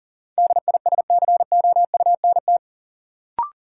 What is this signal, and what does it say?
Morse code, heard as a keyed tone, about 40 wpm, 700 Hz. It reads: DISCOUNT
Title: Sentences from Top 1200 Words - New Word Drill Alt - 40wpm